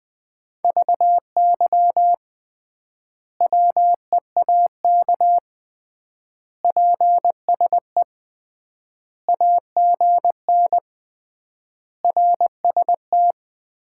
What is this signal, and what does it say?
Morse code, heard as a keyed tone, about 21 wpm, 700 Hz. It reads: VY WEAK PSE AGN RST
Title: Common QSO Phrases - 3x Word-Spacing - 20wpm